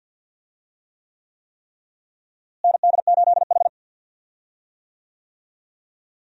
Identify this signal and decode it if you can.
Morse code, heard as a keyed tone, about 46 wpm, 700 Hz. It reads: ND8H
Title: USA Call Signs - 4 Characters - Repeated 3x - 50wpm